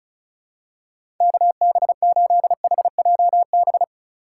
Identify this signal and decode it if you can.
Morse code, heard as a keyed tone, about 34 wpm, 700 Hz. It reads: KB8HJB
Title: USA Call Signs - 6 Characters - 4x Character Spacing - 35wpm